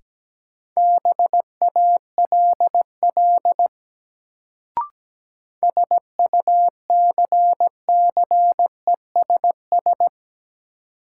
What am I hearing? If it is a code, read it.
Morse code, heard as a keyed tone, about 17 wpm, 700 Hz. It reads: BALL SUCCESS
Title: Sentences from Top 900 Words - New Word Drill - 17wpm